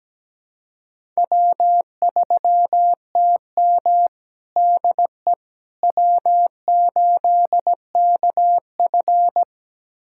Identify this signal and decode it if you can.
Morse code, heard as a keyed tone, about 17 wpm, 700 Hz. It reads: W3TM DE W8KF
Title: USA Callsign DE Callsign - 4 Characters - 17wpm